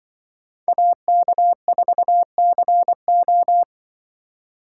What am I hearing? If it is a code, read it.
Morse code, heard as a keyed tone, about 25 wpm, 700 Hz. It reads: AK4CO
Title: Speed Racing - USA 2x2 Call Signs 20wpm